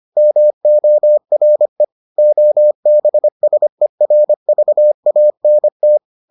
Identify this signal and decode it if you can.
Morse code, heard as a keyed tone, about 25 wpm, 600 Hz. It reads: MORE OBSERVANT